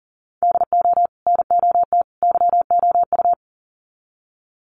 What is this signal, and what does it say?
Morse code, heard as a keyed tone, about 40 wpm, 700 Hz. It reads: DO NOT YOU
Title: Sets of 3 Words 40wpm